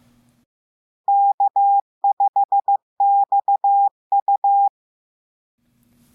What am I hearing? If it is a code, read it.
Morse code, heard as a keyed tone, about 15 wpm, 800 Hz. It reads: K5XU